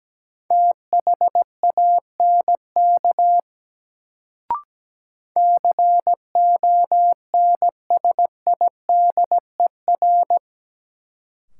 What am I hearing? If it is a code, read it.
Morse code, heard as a keyed tone, about 17 wpm, 700 Hz. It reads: THANK CONSIDER
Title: Sentences from Top 600 Words - New Word Drill - 17wpm